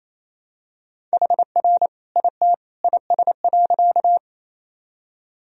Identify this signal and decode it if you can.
Morse code, heard as a keyed tone, about 28 wpm, 700 Hz. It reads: HR IT IS.
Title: Sentences from Top 200 Words and Abbreviations 28wpm